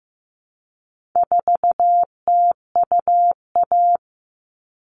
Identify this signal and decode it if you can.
Morse code, heard as a keyed tone, about 15 wpm, 700 Hz. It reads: 4TUA
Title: All Cut Numbers - 4-Digit 15wpm